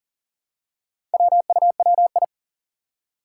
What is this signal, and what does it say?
Morse code, heard as a keyed tone, about 40 wpm, 700 Hz. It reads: WUWI
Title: Four Challenging Letter-Number Combinations 40wpm